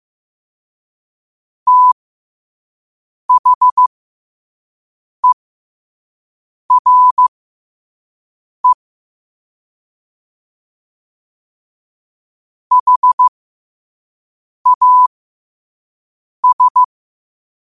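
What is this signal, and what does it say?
Morse code, heard as a keyed tone, about 15 wpm, spaced as for 5 wpm, 1000 Hz. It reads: THERE HAS
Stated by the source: Morse code quote of the day sent at 5 WPM.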